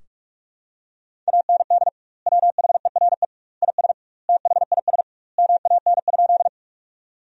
Morse code, heard as a keyed tone, about 46 wpm, 700 Hz. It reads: AND WHERE IS THIS MAN?
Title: 2x Word-Spacing - Sentences from Top 200 Words - 45wpm